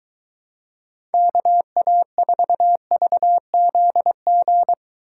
Morse code, heard as a keyed tone, about 23 wpm, 700 Hz. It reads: KA4VZG